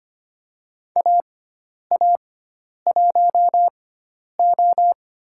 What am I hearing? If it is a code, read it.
Morse code, heard as a keyed tone, about 25 wpm, 700 Hz. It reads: A A 1 O
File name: Lesson 16g - Letter c - Call Signs 25wpm